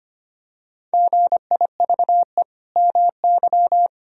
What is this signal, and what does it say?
Morse code, heard as a keyed tone, about 25 wpm, 700 Hz. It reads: GIVE MY
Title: Sentences from Top 1000 Words - Review Part 1 - 25wpm